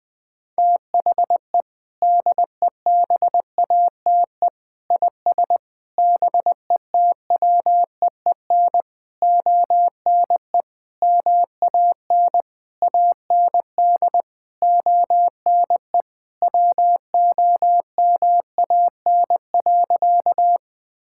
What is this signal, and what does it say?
Morse code, heard as a keyed tone, about 19 wpm, 700 Hz. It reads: THE DEBATE IS BETWEEN ONE MAN AND ONE WOMAN.
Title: Sentences with 6-Word Phrases - Spoken Beginning - 20wpm